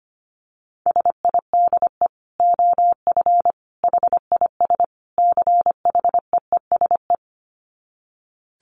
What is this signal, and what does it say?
Morse code, heard as a keyed tone, about 25 wpm, 700 Hz. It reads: SIDE OF HIS CHEESE
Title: Dit Challenge 25wpm